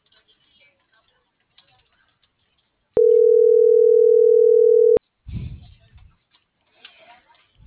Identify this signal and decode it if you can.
Ringback tone (North America), 440 + 480 Hz.